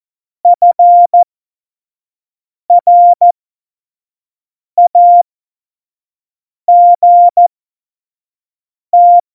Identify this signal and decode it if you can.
Morse code, heard as a keyed tone, about 14 wpm, spaced as for 5 wpm, 700 Hz. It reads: FRAGT
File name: Morsekurs / Witze / Witz 15. Dezember, F70